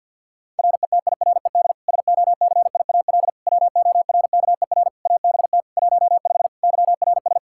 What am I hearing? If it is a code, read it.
Morse code, heard as a keyed tone, about 51 wpm, 700 Hz. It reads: RETIRED SOCIAL WORKER ABT 15 YRS
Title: Practice QSO - Exchange 3 - More Info - 50wpm